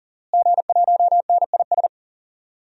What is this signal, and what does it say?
Morse code, heard as a keyed tone, about 40 wpm, 700 Hz. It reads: G1NIS
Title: DXCC-100 Call Signs - 5 Characters - Repeated 3x - 40wpm